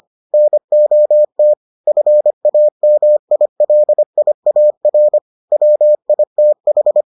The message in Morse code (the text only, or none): NOT FAMILIAR WITH